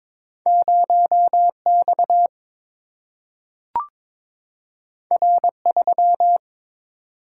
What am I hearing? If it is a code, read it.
Morse code, heard as a keyed tone, about 22 wpm, 700 Hz. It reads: 0X R3
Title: Single-Letter Single-Number 22wpm